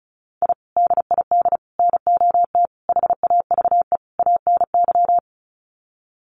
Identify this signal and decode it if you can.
Morse code, heard as a keyed tone, about 34 wpm, 700 Hz. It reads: I DID NOT HAVE ANY
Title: Sets of 5 Words Encore 35wpm